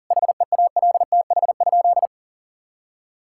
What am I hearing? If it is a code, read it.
Morse code, heard as a keyed tone, about 37 wpm, 700 Hz. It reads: HEALTH?
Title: Sentences from Top 1000 Words - Review Part 5 - 40wpm